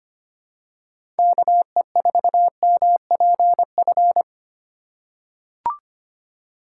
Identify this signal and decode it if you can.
Morse code, heard as a keyed tone, about 25 wpm, 700 Hz. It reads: KE4MPF
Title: USA Call Signs - 6 Characters - 4x Character Spacing - 25wpm